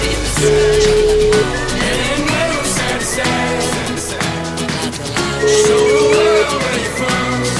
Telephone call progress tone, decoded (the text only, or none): ringback tone